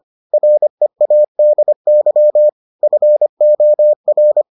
Morse code, heard as a keyed tone, about 25 wpm, 600 Hz. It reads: READY FOR